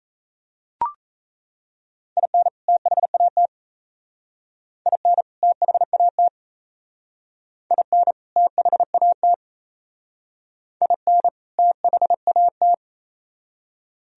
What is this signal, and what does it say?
Morse code, heard as a keyed tone, about 34 wpm, 700 Hz. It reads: IN THAT IN THAT IN THAT IN THAT
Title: Speed-Racing - Sets of 2 Words 28wpm